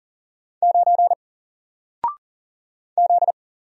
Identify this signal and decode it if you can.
Morse code, heard as a keyed tone, about 40 wpm, 700 Hz. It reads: 9 Z
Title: Single Letter-Number 40wpm